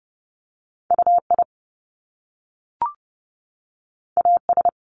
Morse code, heard as a keyed tone, about 30 wpm, 700 Hz. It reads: UI AS
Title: Top Two Letter Combinations in Words 30wpm